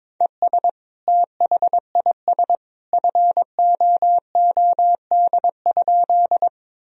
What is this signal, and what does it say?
Morse code, heard as a keyed tone, about 21 wpm, 700 Hz. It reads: ES THIS FOOD?
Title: Sentences from Top 300 Words and Abbreviations 22wpm